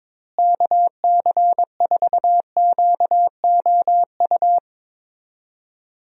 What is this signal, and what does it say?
Morse code, heard as a keyed tone, about 22 wpm, 700 Hz. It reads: KC4QOU
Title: USA Call Signs - 6 Characters - Repeated 2x - 22wpm